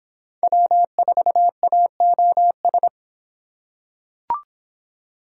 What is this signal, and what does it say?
Morse code, heard as a keyed tone, about 23 wpm, 700 Hz. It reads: W4AOS